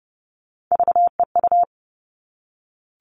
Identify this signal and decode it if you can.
Morse code, heard as a keyed tone, about 30 wpm, 700 Hz. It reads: VEU
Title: All Cut Numbers - 3-Digit 30wpm